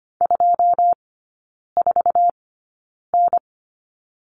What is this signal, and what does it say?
Morse code, heard as a keyed tone, about 25 wpm, spaced as for 9 wpm, 700 Hz. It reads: 24N
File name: Lesson 15e - Number 5 - Three Characters 25wpm